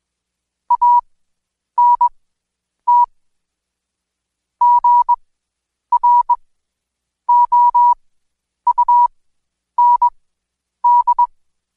Morse code, heard as a keyed tone, about 21 wpm, spaced as for 9 wpm, 970 Hz. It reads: ANT GROUND
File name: Code Quick / AudioDownld2 / Seg115-Day20-DayDrill 20